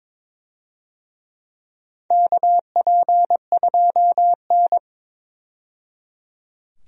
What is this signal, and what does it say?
Morse code, heard as a keyed tone, about 23 wpm, 700 Hz. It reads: KP2N
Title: DXCC-100 Call Signs - 4 Characters - 22wpm